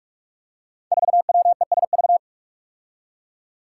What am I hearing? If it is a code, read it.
Morse code, heard as a keyed tone, about 44 wpm, 700 Hz. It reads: 4WESV
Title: Five Challenging Letter-Number Combinations 45wpm